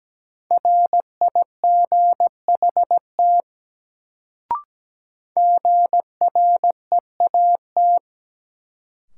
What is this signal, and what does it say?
Morse code, heard as a keyed tone, about 17 wpm, 700 Hz. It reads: RIGHT GREAT
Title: Sets of 2 Words - New Word Drill - 17wpm